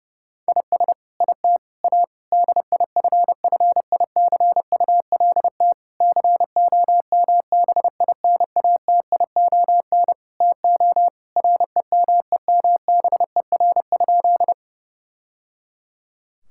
Morse code, heard as a keyed tone, about 30 wpm, 700 Hz. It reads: IS IT A DIFFICULT COMBINATION TO REMEMBER?
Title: Sentences from Top 1000 Words - Review Part 6 - 30wpm